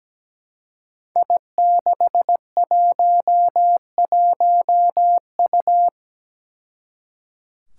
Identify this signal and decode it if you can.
Morse code, heard as a keyed tone, about 17 wpm, 700 Hz. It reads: I611U
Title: Five Challenging Letter-Number Combinations 17wpm